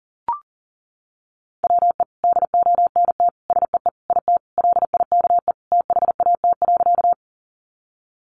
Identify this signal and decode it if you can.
Morse code, heard as a keyed tone, about 40 wpm, 700 Hz. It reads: WE DONT SEE IT LIKE THAT.
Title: Sentences from Top 300 Words 40wpm